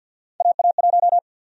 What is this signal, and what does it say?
Morse code, heard as a keyed tone, about 50 wpm, 700 Hz. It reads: AA1
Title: USA Call Sign Prefixes - 50wpm